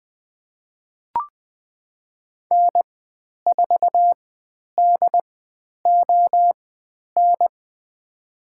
Morse code, heard as a keyed tone, about 20 wpm, spaced as for 10 wpm, 700 Hz. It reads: N4DON
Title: 102 Lesson 17g - Letter u - Call Signs 20wpm